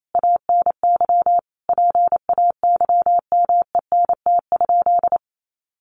Morse code, heard as a keyed tone, about 28 wpm, 700 Hz. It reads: ANY PAYMENT?
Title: Sentences from Top 1200 Words 28wpm